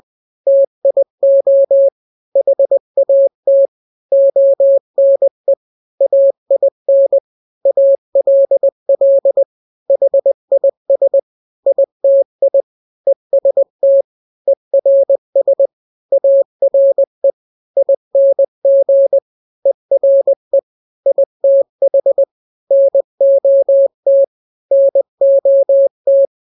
Morse code, heard as a keyed tone, about 20 wpm, 550 Hz. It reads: TIO HAT ONE AIN ALL HIS ITI EST ERS ARE ING ERE ITH NOT NOT
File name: n-grams / 3-grams 20 18 WPM